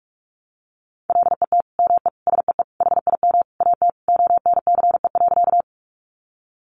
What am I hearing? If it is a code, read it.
Morse code, heard as a keyed tone, about 44 wpm, 700 Hz. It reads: LET ME SEE HIM AT ONCE.
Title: Sentences from Top 300 Words 45wpm